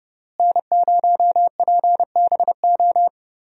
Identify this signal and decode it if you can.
Morse code, heard as a keyed tone, about 30 wpm, 700 Hz. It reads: N0PBO
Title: USA Call Signs - 5 Characters - Repeated 2x - 30wpm